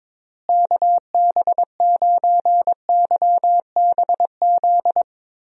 Morse code, heard as a keyed tone, about 22 wpm, 700 Hz. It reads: KB9YBZ
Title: USA Call Signs - 6 Characters - Repeated 3x - 22wpm